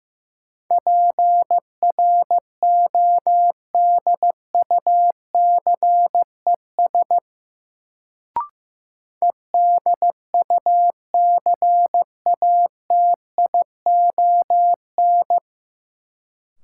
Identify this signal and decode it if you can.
Morse code, heard as a keyed tone, about 15 wpm, 700 Hz. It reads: PRODUCES EDUCATION
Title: Sentences from Top 1000 Words - Review Part 6 - New Word Drill - 15wpm